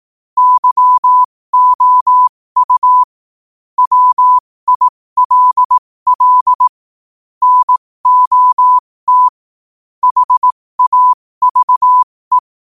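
Morse code, heard as a keyed tone, about 18 wpm, spaced as for 15 wpm, 1000 Hz. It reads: YOU WILL NOT HAVE
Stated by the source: Morse quote-of-the-day practice at 15 WPM from Morse Resource.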